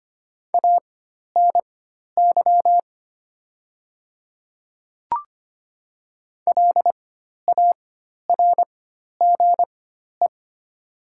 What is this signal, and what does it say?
Morse code, heard as a keyed tone, about 25 wpm, spaced as for 12 wpm, 700 Hz. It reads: ANY LARGE
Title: 4x Character Spacing - Top 200 Words - 25wpm